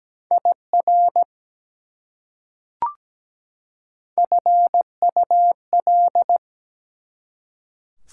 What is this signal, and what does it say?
Morse code, heard as a keyed tone, about 17 wpm, 700 Hz. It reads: IR FUL
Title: Prefixes and Suffixes 17wpm